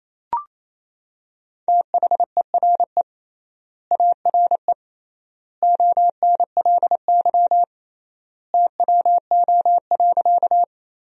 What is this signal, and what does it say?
Morse code, heard as a keyed tone, about 27 wpm, 700 Hz. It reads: THERE ARE ONLY TWO.
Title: 3x Word-Spacing - Sentences from Top 200 Words - 28wpm